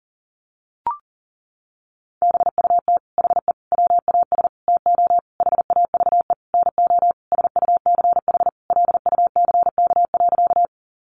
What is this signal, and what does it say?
Morse code, heard as a keyed tone, about 40 wpm, 700 Hz. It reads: BUT HE WAS TO HAVE NO SUCH LUCK.
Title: Sentences from Top 1500 Words 40wpm